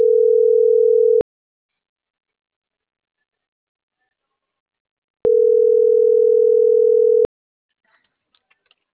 Ringback tone (North America), 440 + 480 Hz.